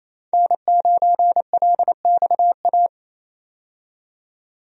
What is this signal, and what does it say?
Morse code, heard as a keyed tone, about 28 wpm, 700 Hz. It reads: N9LXA
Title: USA Call Signs - 5 Characters - 28wpm